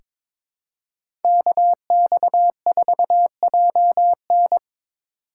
Morse code, heard as a keyed tone, about 22 wpm, 700 Hz. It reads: KX4JN
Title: USA Call Signs - 5 Characters - 4x Character Spacing - 22wpm